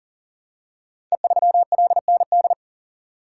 Morse code, heard as a keyed tone, about 39 wpm, 700 Hz. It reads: E3LND
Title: Code Groups - 5 Letter:Numbers - 40wpm